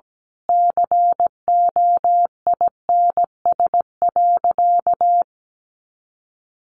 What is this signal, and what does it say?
Morse code, heard as a keyed tone, about 17 wpm, 700 Hz. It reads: COINS.
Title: A Sound Of Thunder 17wpm